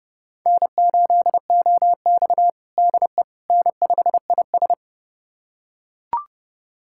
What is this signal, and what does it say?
Morse code, heard as a keyed tone, about 30 wpm, 700 Hz. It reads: N8OX DE N5IS
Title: USA Callsign DE Callsign - 4 Characters - 30wpm